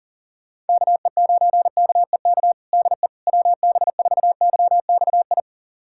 Morse code, heard as a keyed tone, about 40 wpm, 700 Hz. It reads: KE9KEK DE WB4YXI
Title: USA Callsign DE Callsign - 6 Characters - 40wpm